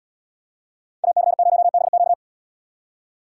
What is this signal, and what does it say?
Morse code, heard as a keyed tone, about 78 wpm, 700 Hz. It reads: AC0BQ